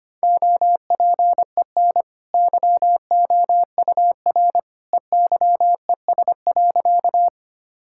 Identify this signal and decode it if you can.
Morse code, heard as a keyed tone, about 25 wpm, 690 Hz. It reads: OPEN YOUR EYES.
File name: Easy Sentences with Hints - 25wpm